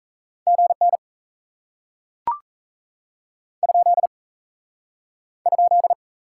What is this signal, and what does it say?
Morse code, heard as a keyed tone, about 37 wpm, 700 Hz. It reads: GN ? ?